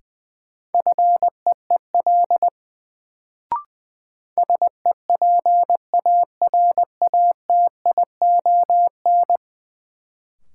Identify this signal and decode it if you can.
Morse code, heard as a keyed tone, about 20 wpm, 700 Hz. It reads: FEEL SEPARATION
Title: Sets of 5 Words - New Word Drill - 20wpm